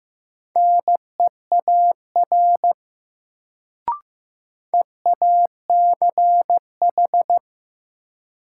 Morse code, heard as a keyed tone, about 15 wpm, 700 Hz. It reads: NEAR EACH
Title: Sentences from Top 200 Words - New Word Drill - 15wpm